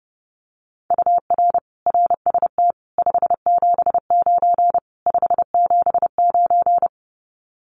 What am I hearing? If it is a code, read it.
Morse code, heard as a keyed tone, about 30 wpm, 700 Hz. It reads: UR RST 579 579
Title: Common QSO Phrases Encore 30wpm